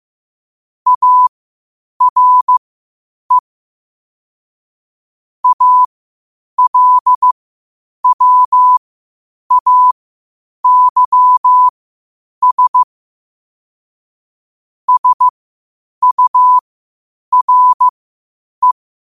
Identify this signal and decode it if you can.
Morse code, heard as a keyed tone, about 15 wpm, spaced as for 8 wpm, 1000 Hz. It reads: ARE ALWAYS SURE
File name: qotdfiles / 8wpm